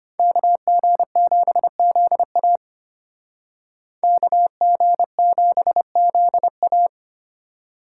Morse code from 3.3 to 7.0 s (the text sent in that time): KG7ZA